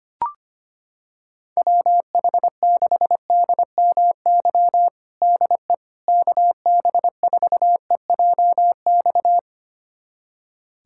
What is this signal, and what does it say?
Morse code, heard as a keyed tone, about 25 wpm, 700 Hz. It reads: WH6DMY DE KB4EJX
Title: USA Callsign DE Callsign - 6 Characters - 25wpm